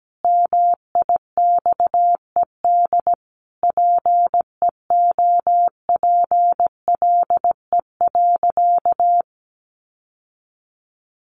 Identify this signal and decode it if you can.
Morse code, heard as a keyed tone, about 17 wpm, 700 Hz. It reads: MIXED PEOPLE.
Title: Sentences from Top 1500 Words 17wpm